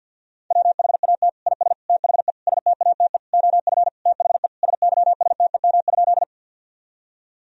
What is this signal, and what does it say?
Morse code, heard as a keyed tone, about 52 wpm, 700 Hz. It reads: WHAT IS THE STATE OF THE SYSTEM?
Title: Sentences from Top 1000 Words - Review Part 3 - 50wpm